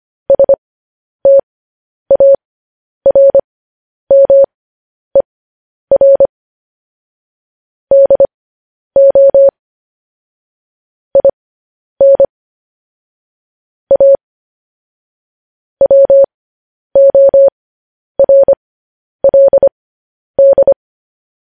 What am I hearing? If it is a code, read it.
Morse code, heard as a keyed tone, about 25 wpm, spaced as for 10 wpm, 550 Hz. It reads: STARMER DO IN A WORLD
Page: news headlines in morse code at 10 wpm